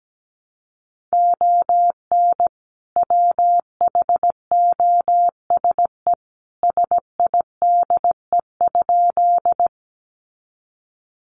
Morse code, heard as a keyed tone, about 17 wpm, 700 Hz. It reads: ON WHOSE SIDE?
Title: Sentences from Top 1200 Words 17wpm